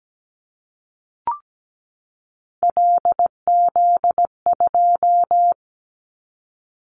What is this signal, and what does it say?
Morse code, heard as a keyed tone, about 17 wpm, 700 Hz. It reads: LZ2
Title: Callsign Prefixes for Top 50 DXCC - 17wpm